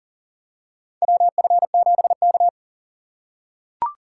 Morse code, heard as a keyed tone, about 38 wpm, 700 Hz. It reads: WF7K
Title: USA Call Signs - 3x Character Spacing - 40wpm